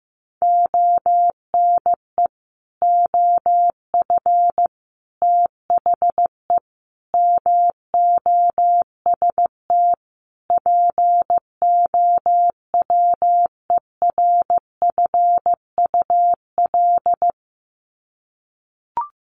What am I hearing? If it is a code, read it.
Morse code, heard as a keyed tone, about 15 wpm, 700 Hz. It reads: ONE OF THE MOST POWERFUL
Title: Sets of 5 Words Encore 15wpm